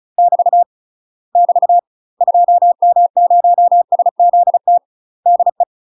Morse code, heard as a keyed tone, about 34 wpm, 700 Hz. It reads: <BT> <BT> 2M0SZT DE
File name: UK MASTER QSOs TRAINING / 35WPM / UK MASTER QSOs 0025